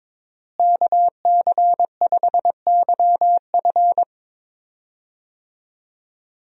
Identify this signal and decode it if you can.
Morse code, heard as a keyed tone, about 22 wpm, 700 Hz. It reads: KC5YF
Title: USA Call Signs - 5 Characters - Repeated 2x - 22wpm